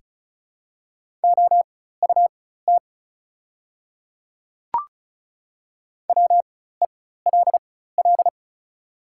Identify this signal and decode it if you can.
Morse code, heard as a keyed tone, about 34 wpm, spaced as for 16 wpm, 690 Hz. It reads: OUT WELL
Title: 4x Character Spacing - Top 300 Words - 35wpm